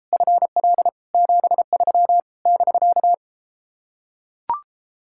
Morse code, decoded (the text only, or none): FL 73 <BK>